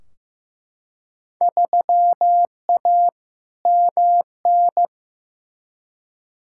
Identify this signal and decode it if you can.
Morse code, heard as a keyed tone, about 15 wpm, 700 Hz. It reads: 3A MN
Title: Field Day Exchanges - 15wpm